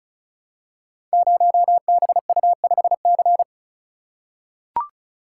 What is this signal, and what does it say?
Morse code, heard as a keyed tone, about 36 wpm, 700 Hz. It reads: 0BU5C
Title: Code Groups - 5 Letter:Numbers - 35wpm